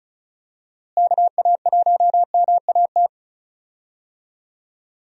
Morse code, heard as a keyed tone, about 34 wpm, 700 Hz. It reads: KA1MAT
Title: USA Call Signs - 6 Characters - Repeated 3x - 35wpm